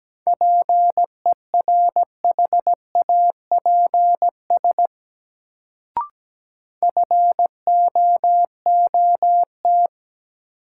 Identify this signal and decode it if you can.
Morse code, heard as a keyed tone, about 17 wpm, 700 Hz. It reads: PERHAPS FOOT
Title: Sentences from Top 500 Words - New Word Drill - 17wpm